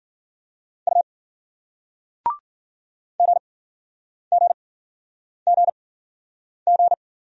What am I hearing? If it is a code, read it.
Morse code, heard as a keyed tone, about 48 wpm, spaced as for 9 wpm, 700 Hz. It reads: U GGGG
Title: Speed-Racing Single Letters 40wpm